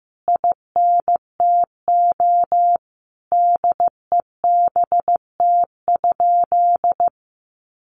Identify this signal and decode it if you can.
Morse code, heard as a keyed tone, about 15 wpm, 700 Hz. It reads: INTO DEBT?
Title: Sentences from Top 1300 Words 15wpm